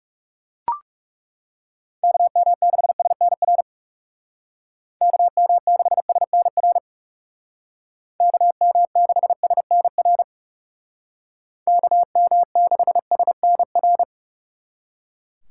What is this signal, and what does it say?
Morse code, heard as a keyed tone, about 36 wpm, 700 Hz. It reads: KM6SNR KM6SNR KM6SNR KM6SNR
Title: Speed Racing - USA 2x3 Call Signs 30wpm